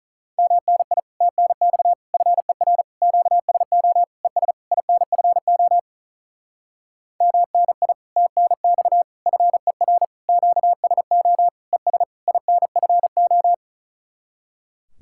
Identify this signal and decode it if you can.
Morse code, heard as a keyed tone, about 46 wpm, 700 Hz. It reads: MNI TNX FER QSO ES INFO MNI TNX FER QSO ES INFO